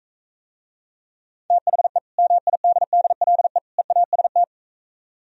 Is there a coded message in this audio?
Yes: Morse code from 1.5 to 4.4 s, heard as a keyed tone, about 42 wpm, 700 Hz. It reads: THE MIDDLE EAST